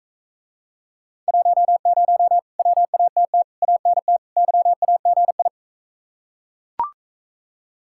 Morse code, heard as a keyed tone, about 41 wpm, 700 Hz. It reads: 10 WATT ANT YAGI